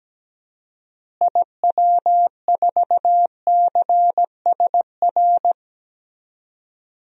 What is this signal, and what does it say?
Morse code, heard as a keyed tone, about 17 wpm, 700 Hz. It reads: IW4CSR
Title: DXCC-100 Call Signs - 6 Characters - Repeated 3x - 17wpm